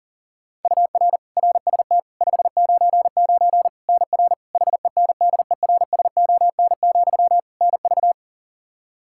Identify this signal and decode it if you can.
Morse code, heard as a keyed tone, about 40 wpm, 700 Hz. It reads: UR RST 599 NR HENDERSON, NV
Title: Practice QSO - Exchange 1 - Beginning - 40wpm